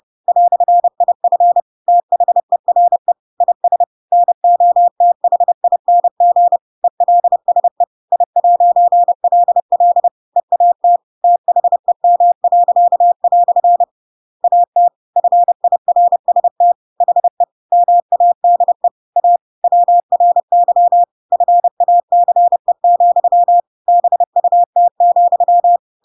Morse code, heard as a keyed tone, about 30 wpm, 700 Hz. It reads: "IF THERE IS NOTHING ELSE I'LL EAT THEM." AT FIRST HE MADE A WRY FACE, BUT,
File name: Pinocchio / Pinocchio ch7 30 30 ik1hge